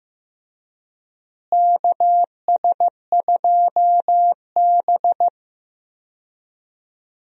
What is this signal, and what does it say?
Morse code, heard as a keyed tone, about 15 wpm, 700 Hz. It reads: KS2B
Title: USA Call Signs - 4 Characters - Repeated 3x - 15wpm